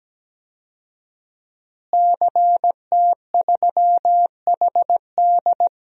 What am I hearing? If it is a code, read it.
Morse code, heard as a keyed tone, about 17 wpm, 700 Hz. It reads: CT3HD
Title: DXCC-100 Call Signs - 5 Characters - Repeated 2x - 17wpm